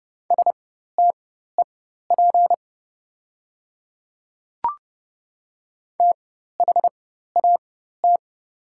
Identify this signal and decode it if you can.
Morse code, heard as a keyed tone, about 30 wpm, spaced as for 14 wpm, 690 Hz. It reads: STEP THAT
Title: 4x Character Spacing - Top 500 Words - 30wpm